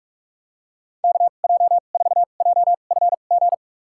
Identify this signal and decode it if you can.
Morse code, heard as a keyed tone, about 46 wpm, spaced as for 32 wpm, 700 Hz. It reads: KJ4JFG
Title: USA Call Signs - 6 Characters - 2x Character Spacing - 45wpm